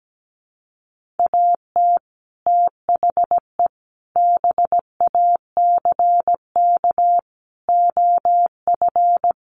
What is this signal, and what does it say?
Morse code, heard as a keyed tone, about 17 wpm, 700 Hz. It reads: AT THE BACK OF
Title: Sets of 5 Words 17wpm